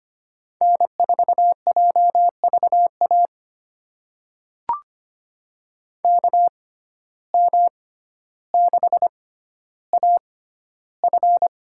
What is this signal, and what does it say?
Morse code, heard as a keyed tone, about 25 wpm, spaced as for 13 wpm, 700 Hz. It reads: N4JVA KM6AF
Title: USA Call Signs - 5 Characters - 6x Character Spacing - 25wpm